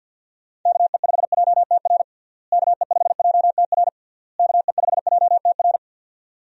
Morse code, heard as a keyed tone, about 50 wpm, 700 Hz. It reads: KE5JTR KE5JTR KE5JTR
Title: USA Call Signs - 6 Characters - Repeated 3x - 50wpm